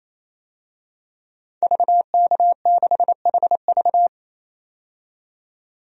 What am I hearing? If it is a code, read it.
Morse code, heard as a keyed tone, about 28 wpm, 700 Hz. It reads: VK6HV
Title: DXCC-100 Call Signs - 5 Characters - 28wpm